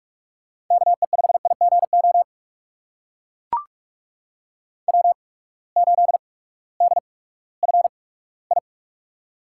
Morse code, heard as a keyed tone, about 44 wpm, spaced as for 20 wpm, 700 Hz. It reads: KE5IGO W8DFI
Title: USA Call Signs - 8x Character Spacing - 45wpm